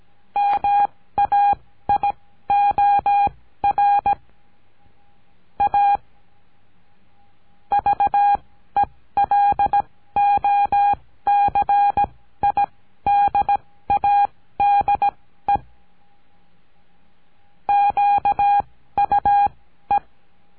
Morse code, heard as a keyed tone, about 17 wpm, spaced as for 13 wpm, 810 Hz. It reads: MAIOR A VELOCIDADE QUE
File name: CURSO DE TELEGRAFIA